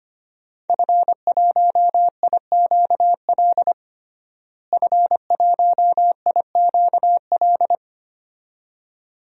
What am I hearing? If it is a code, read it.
Morse code, heard as a keyed tone, about 25 wpm, 700 Hz. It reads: F1IQL F1IQL
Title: DXCC-100 Call Signs - 5 Characters - Repeated 2x - 25wpm